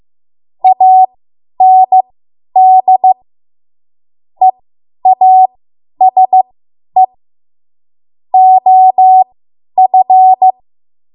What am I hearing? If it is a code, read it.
Morse code, heard as a keyed tone, about 15 wpm, spaced as for 10 wpm, 750 Hz. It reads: AND EASE OF
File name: Morse / 10WPM